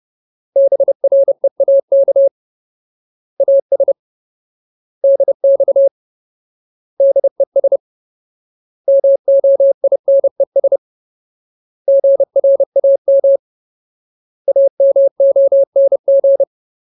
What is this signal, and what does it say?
Morse code, heard as a keyed tone, about 30 wpm, 550 Hz. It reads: BREAK AS DX DES MOINES GRAM AMONG
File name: Ultimate Word List / 30WPM Ultimate